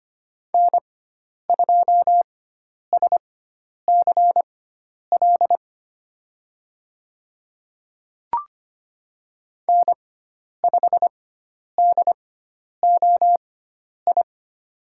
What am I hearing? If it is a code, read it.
Morse code, heard as a keyed tone, about 25 wpm, spaced as for 10 wpm, 700 Hz. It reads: N2SCL N5DOI
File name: Lesson 16g - Letter c - Call Signs 25wpm